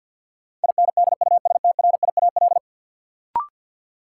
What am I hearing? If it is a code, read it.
Morse code, heard as a keyed tone, about 53 wpm, 700 Hz. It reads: INDUSTRIAL
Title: Sets of 5 Words - New Word Drill - 50wpm